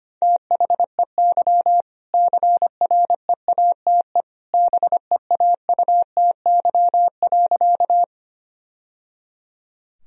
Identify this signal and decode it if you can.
Morse code, heard as a keyed tone, about 25 wpm, 700 Hz. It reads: THEY CREATE BEAUTY.
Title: Sentences from Top 1000 Words - Review Part 3 - 25wpm